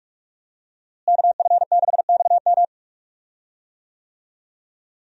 Morse code, heard as a keyed tone, about 44 wpm, 700 Hz. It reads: KF6XM
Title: USA Call Signs - 5 Characters - Repeated 3x - 45wpm